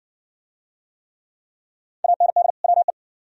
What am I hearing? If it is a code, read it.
Morse code, heard as a keyed tone, about 56 wpm, 700 Hz. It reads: AND WE